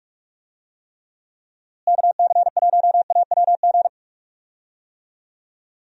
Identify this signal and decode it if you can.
Morse code, heard as a keyed tone, about 41 wpm, 700 Hz. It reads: KC1AWG
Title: USA Call Signs - 6 Characters - Repeated 3x - 45wpm